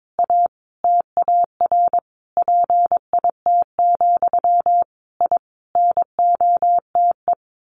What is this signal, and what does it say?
Morse code, heard as a keyed tone, about 22 wpm, 700 Hz. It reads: A TAR PIT, I NOTE
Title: A Sound Of Thunder 22wpm